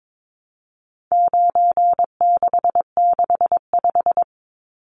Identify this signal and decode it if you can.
Morse code, heard as a keyed tone, about 22 wpm, 700 Hz. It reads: 9665
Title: Numbers - 4-Digit 22wpm